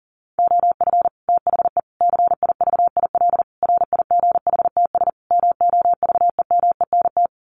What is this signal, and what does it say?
Morse code, heard as a keyed tone, about 40 wpm, 700 Hz. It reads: OF THE CIVIL RIGHTS MOVEMENT
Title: Sets of 5 Words Encore2 40wpm